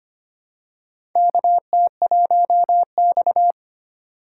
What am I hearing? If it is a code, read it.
Morse code, heard as a keyed tone, about 25 wpm, 700 Hz. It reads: KT1X
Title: USA Call Signs - 4 Characters - 4x Character Spacing - 25wpm